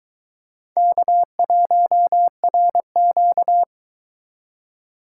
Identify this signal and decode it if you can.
Morse code, heard as a keyed tone, about 20 wpm, 700 Hz. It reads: K1RQ
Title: Speed Racing - USA Call Signs 17wpm